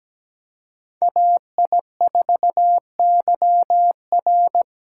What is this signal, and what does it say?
Morse code, heard as a keyed tone, about 20 wpm, 700 Hz. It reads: AI4YR
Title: Speed Racing - USA Call Signs 17wpm